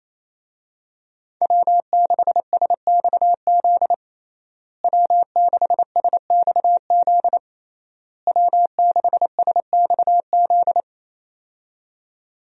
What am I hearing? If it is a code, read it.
Morse code, heard as a keyed tone, about 28 wpm, 700 Hz. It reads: W6SXZ W6SXZ W6SXZ
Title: USA Call Signs - 5 Characters - Repeated 3x - 28wpm